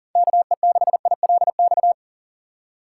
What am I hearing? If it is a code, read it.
Morse code, heard as a keyed tone, about 40 wpm, 700 Hz. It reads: KE6ILX
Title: USA Call Signs - 6 Characters - 2x Character Spacing - 40wpm